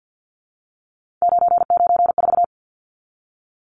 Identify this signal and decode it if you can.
Morse code, heard as a keyed tone, about 50 wpm, 700 Hz. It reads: 994
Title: Numbers - 3-Digit 50wpm